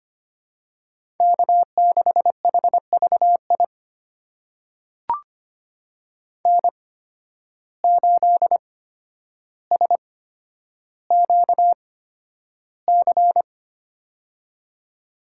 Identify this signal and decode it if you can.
Morse code, heard as a keyed tone, about 25 wpm, spaced as for 11 wpm, 700 Hz. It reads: K6HVI N8SQC
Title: USA Call Signs - 5 Characters - 8x Character Spacing - 25wpm